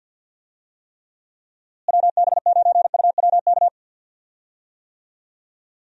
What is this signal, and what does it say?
Morse code, heard as a keyed tone, about 50 wpm, 700 Hz. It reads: WB9UWK
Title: USA Call Signs - 6 Characters - Repeated 2x - 50wpm